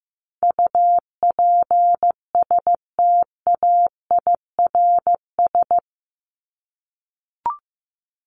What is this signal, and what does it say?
Morse code, heard as a keyed tone, about 15 wpm, 700 Hz. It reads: UPSTAIRS
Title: Top 1751-2000 Words 15wpm